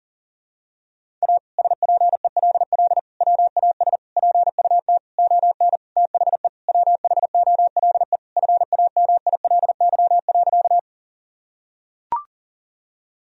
Morse code, heard as a keyed tone, about 40 wpm, 700 Hz. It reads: A SPELL WAS PUT ON THE WHOLE FAMILY.
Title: Sentences from Top 1000 Words - Review Part 1 - 40wpm